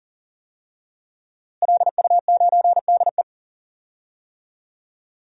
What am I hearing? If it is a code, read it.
Morse code, heard as a keyed tone, about 40 wpm, 700 Hz. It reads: LU9DE
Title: DXCC-100 Call Signs - 5 Characters - Repeated 3x - 40wpm